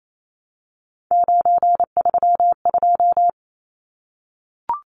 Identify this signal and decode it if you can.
Morse code, heard as a keyed tone, about 28 wpm, 700 Hz. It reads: 932
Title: Numbers - 3-Digit 28wpm